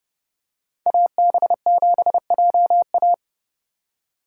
Morse code, heard as a keyed tone, about 30 wpm, 700 Hz. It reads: AB7JA
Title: USA Call Signs - 5 Characters - 8x Character Spacing - 30wpm